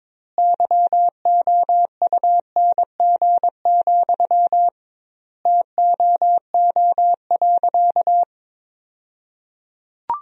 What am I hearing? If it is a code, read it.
Morse code, heard as a keyed tone, about 22 wpm, 700 Hz. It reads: YOUNG, TOO.
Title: 2x Word-Spacing - Sentences from Top 400 Words - 22wpm